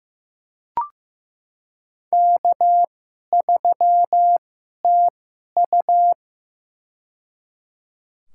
Morse code, heard as a keyed tone, about 15 wpm, spaced as for 11 wpm, 700 Hz. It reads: K3TU
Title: USA Call Signs - 4 Characters - 2x Character Spacing - 15wpm